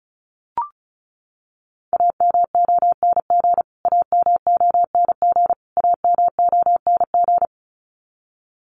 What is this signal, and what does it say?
Morse code, heard as a keyed tone, about 34 wpm, 700 Hz. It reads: AMONG AMONG AMONG
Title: Top 500 Words Repeated 35wpm